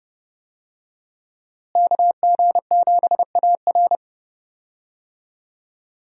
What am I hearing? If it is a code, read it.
Morse code, heard as a keyed tone, about 30 wpm, 700 Hz. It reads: KG7AR
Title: USA Call Signs - 5 Characters - Repeated 3x - 30wpm